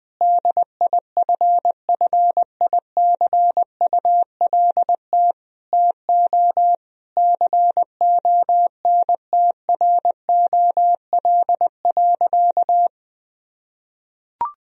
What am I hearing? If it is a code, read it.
Morse code, heard as a keyed tone, about 20 wpm, 700 Hz. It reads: DIFFICULT TO CONTROL.
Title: Sentences from Top 1000 Words - Review Part 4 - 20wpm